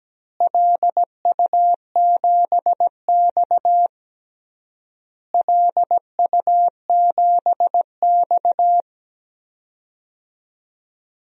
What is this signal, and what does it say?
Morse code, heard as a keyed tone, about 17 wpm, 700 Hz. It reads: LU7X LU7X
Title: DXCC-100 Call Signs - 4 Characters - Repeated 3x - 17wpm